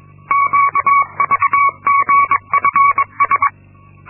Morse code, heard as a keyed tone, about 22 wpm, 1120 Hz. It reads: QVGFS
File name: Linksmakalnis-2